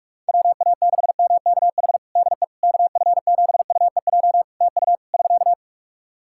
Morse code, heard as a keyed tone, about 41 wpm, 700 Hz. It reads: WA6MKH DE KF7UEJ TU <SK>
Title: K1USN SST QSO Practice 45wpm